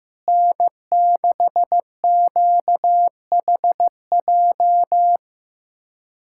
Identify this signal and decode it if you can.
Morse code, heard as a keyed tone, about 15 wpm, 700 Hz. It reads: N6QHJ
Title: USA Callsign DE Callsign - 5 Characters - 15wpm